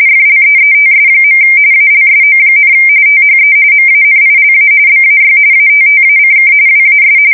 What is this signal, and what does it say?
RTTY, about 45.45 baud. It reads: FLEETS. ALSO, THE MANILA BAY ATTACK WAS
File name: fsk170 / usnavy3